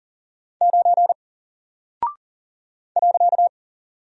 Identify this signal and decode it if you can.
Morse code, heard as a keyed tone, about 40 wpm, 700 Hz. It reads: 9 .
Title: Single Character (Letter:Number:Punctuation:Prosign) Alt 40wpm